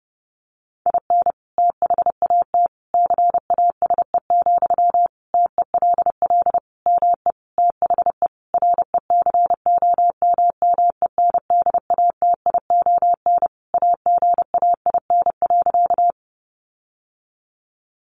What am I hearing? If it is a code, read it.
Morse code, heard as a keyed tone, about 30 wpm, 700 Hz. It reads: IN THAT CASE, TELL ME THE RECOMMENDATION AGAIN.
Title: Sentences from Top 1400 Words 30wpm